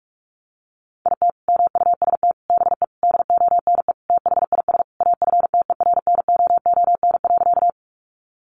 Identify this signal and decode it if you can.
Morse code, heard as a keyed tone, about 44 wpm, 700 Hz. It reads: IT MUST BE DONE THIS AFTERNOON.
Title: Sentences from Top 1500 Words 45wpm